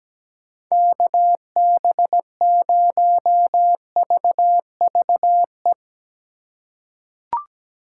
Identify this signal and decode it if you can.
Morse code, heard as a keyed tone, about 17 wpm, 700 Hz. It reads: KB0VVE
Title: USA Call Signs - 6 Characters - 4x Character Spacing - 17wpm